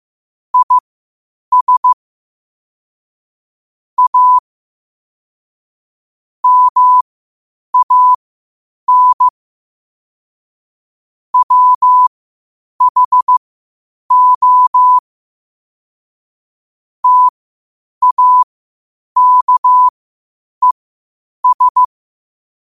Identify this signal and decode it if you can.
Morse code, heard as a keyed tone, about 15 wpm, spaced as for 8 wpm, 1000 Hz. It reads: IS A MAN WHO TAKES
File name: qotdfiles / 8wpm